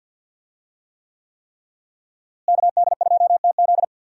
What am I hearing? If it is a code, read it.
Morse code, heard as a keyed tone, about 49 wpm, 700 Hz. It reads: KD2TZ
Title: USA Call Signs - 5 Characters - Repeated 3x - 50wpm